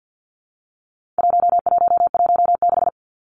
Morse code, heard as a keyed tone, about 50 wpm, 700 Hz. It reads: 1116
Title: Numbers - 4-Digit 50wpm